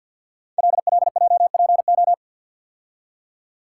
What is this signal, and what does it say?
Morse code, heard as a keyed tone, about 53 wpm, 700 Hz. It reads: LLJPO
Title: Code Groups - 5 Letters - 50wpm